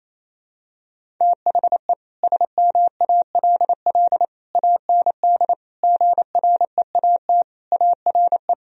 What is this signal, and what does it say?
Morse code, heard as a keyed tone, about 28 wpm, 700 Hz. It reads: THE SMALL AND GREAT ARE
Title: Sentences from Top 1000 Words - Review Part 1 - 28wpm